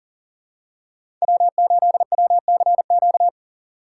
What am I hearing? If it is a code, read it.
Morse code, heard as a keyed tone, about 40 wpm, 700 Hz. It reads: W8WCQ
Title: USA Call Signs - 5 Characters - 6x Character Spacing - 40wpm